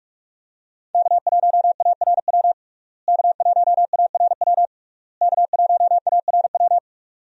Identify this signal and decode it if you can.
Morse code, heard as a keyed tone, about 46 wpm, 700 Hz. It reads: K1ARW K1ARW K1ARW
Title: USA Call Signs - 5 Characters - Repeated 3x - 45wpm